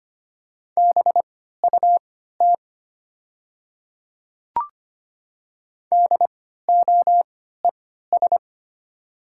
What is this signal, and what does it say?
Morse code, heard as a keyed tone, about 25 wpm, spaced as for 14 wpm, 700 Hz. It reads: BUT DOES
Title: 3x Character Spacing - Top 500 Words - 25wpm